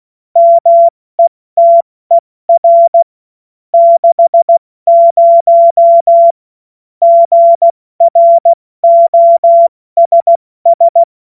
Morse code, heard as a keyed tone, about 16 wpm, 680 Hz. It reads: METER 60 GROSS
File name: Morsekurs / Witze / Witz 14. September 2021, 70 F80